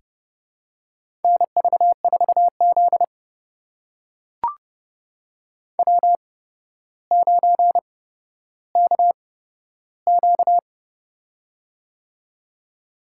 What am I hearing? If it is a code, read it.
Morse code, heard as a keyed tone, about 31 wpm, spaced as for 13 wpm, 700 Hz. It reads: NV4Z W9KQ
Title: USA Call Signs - 4 Characters - 8x Character Spacing - 30wpm